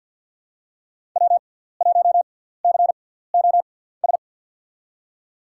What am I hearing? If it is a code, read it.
Morse code, heard as a keyed tone, about 53 wpm, spaced as for 17 wpm, 700 Hz. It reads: W1COS
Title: USA Call Signs - 6x Character Spacing - 50wpm